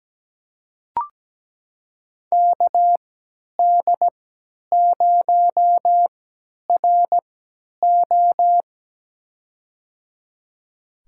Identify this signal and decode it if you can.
Morse code, heard as a keyed tone, about 17 wpm, spaced as for 10 wpm, 700 Hz. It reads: KD0RO
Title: USA Call Signs - 3x Character Spacing - 17wpm